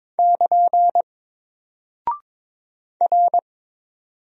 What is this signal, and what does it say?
Morse code, heard as a keyed tone, about 22 wpm, 700 Hz. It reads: <KN> R
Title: Single Character (Letter:Number:Punctuation:Prosign) Alt 22wpm